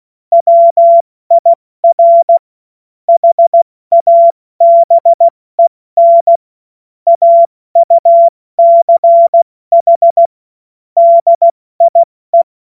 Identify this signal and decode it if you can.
Morse code, heard as a keyed tone, about 16 wpm, 680 Hz. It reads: WIR HABEN AUCH DIE